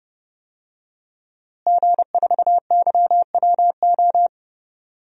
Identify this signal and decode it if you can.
Morse code, heard as a keyed tone, about 30 wpm, 700 Hz. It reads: G4YWO
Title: DXCC-100 Call Signs - 5 Characters - Repeated 3x - 30wpm